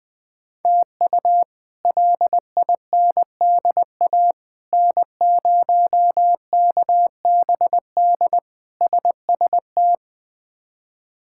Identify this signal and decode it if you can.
Morse code, heard as a keyed tone, about 20 wpm, 700 Hz. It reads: TU LINDA N0KBD SST
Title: K1USN SST QSO Practice - 20wpm